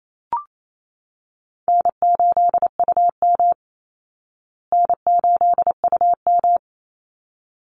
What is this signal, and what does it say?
Morse code, heard as a keyed tone, about 28 wpm, 700 Hz. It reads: N8UM N8UM
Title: Easy Call Signs Repeated 2x - 28wpm